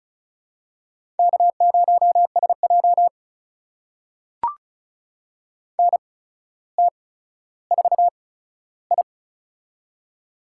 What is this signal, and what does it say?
Morse code, heard as a keyed tone, about 34 wpm, spaced as for 15 wpm, 700 Hz. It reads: K0SJ NT4I
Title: USA Call Signs - 4 Characters - 8x Character Spacing - 35wpm